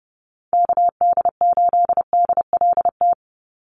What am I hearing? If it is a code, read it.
Morse code, heard as a keyed tone, about 30 wpm, 700 Hz. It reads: KD8DLT
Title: USA Call Signs 30wpm